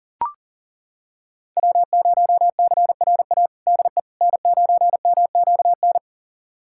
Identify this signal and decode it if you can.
Morse code, heard as a keyed tone, about 40 wpm, 700 Hz. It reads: W0CRA DE N9MQN
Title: USA Callsign DE Callsign - 5 Characters - 40wpm